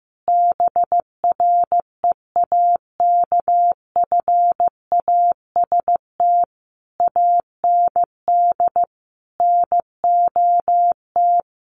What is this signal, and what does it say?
Morse code, heard as a keyed tone, about 15 wpm, 700 Hz. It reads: BREAKFAST AND NOT
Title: Sentences from Top 1500 Words 15wpm